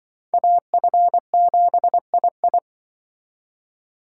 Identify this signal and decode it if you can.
Morse code, heard as a keyed tone, about 23 wpm, 700 Hz. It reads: AF7II
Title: Speed Racing - USA 2x2 Call Signs 20wpm